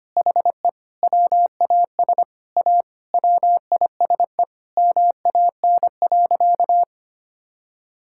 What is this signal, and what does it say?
Morse code, heard as a keyed tone, about 25 wpm, 700 Hz. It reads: HE WAS A WISE MAN.
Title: Sentences from Top 1600 Words 25wpm